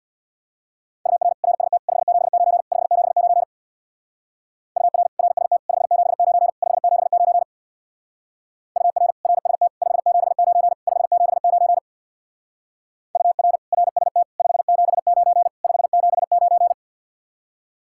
Morse code, heard as a keyed tone, about 60 wpm, 700 Hz. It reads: UR RST 579 579 UR RST 579 579 UR RST 579 579 UR RST 579 579
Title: Speed-Racing Common QSO Phrases - 50wpm